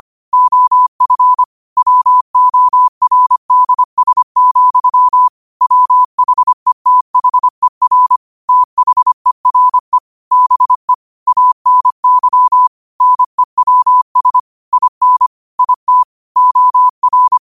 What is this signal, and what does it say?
Morse code, heard as a keyed tone, about 25 wpm, 1000 Hz. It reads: OF WORDS, WHETHER THERE BE ANY NEWS IN IT OR
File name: 25wpm